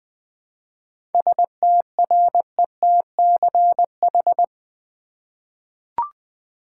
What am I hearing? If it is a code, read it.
Morse code, heard as a keyed tone, about 20 wpm, 700 Hz. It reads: STRETCH
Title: Seven Letter Words - 20wpm